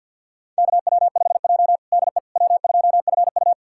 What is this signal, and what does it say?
Morse code, heard as a keyed tone, about 51 wpm, 690 Hz. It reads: KW5J DE W2FU
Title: USA Callsign DE Callsign - 4 Characters - 50wpm